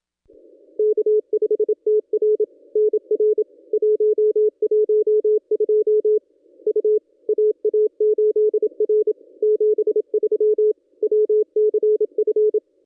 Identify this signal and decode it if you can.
Morse code, heard as a keyed tone, about 26 wpm, 470 Hz. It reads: K5TR NR 112 U AA8R 73 WCF
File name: contests / ss-cw-k5tr-op-n5rz -1100